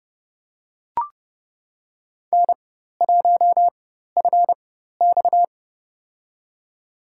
Morse code, heard as a keyed tone, about 30 wpm, spaced as for 14 wpm, 700 Hz. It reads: N1FX
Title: USA Call Signs - 4 Characters - 4x Character Spacing - 30wpm